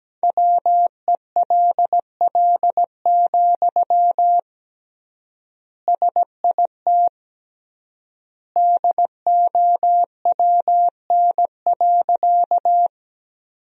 Morse code, heard as a keyed tone, about 17 wpm, 700 Hz. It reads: WELL, SIT DOWN.
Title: 3x Word-Spacing - Sentences from Top 500 Words - 17wpm